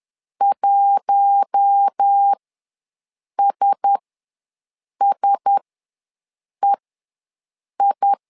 Morse code, heard as a keyed tone, about 10 wpm, spaced as for 6 wpm, 800 Hz. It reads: 1SSEI
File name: LEZIONI / 4 Q1S ws 10wpm 2208